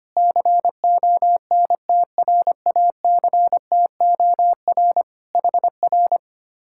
Morse code, heard as a keyed tone, about 24 wpm, 700 Hz. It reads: CONTRACTOR HR
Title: Speed-Racing Common QSO Phrases - 25wpm